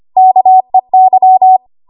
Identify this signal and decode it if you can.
Morse code, heard as a keyed tone, about 25 wpm, 750 Hz. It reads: KEY
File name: Archive / 25 WPM / 25WPM